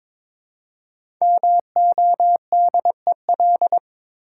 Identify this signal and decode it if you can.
Morse code, heard as a keyed tone, about 22 wpm, 700 Hz. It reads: MODEL
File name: 5 Letter Words - First 3 Letters Given - 22wpm